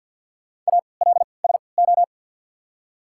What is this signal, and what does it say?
Morse code, heard as a keyed tone, about 53 wpm, spaced as for 29 wpm, 700 Hz. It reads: ALSO
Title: 3x Character Spacing - Top 200 Words - 50wpm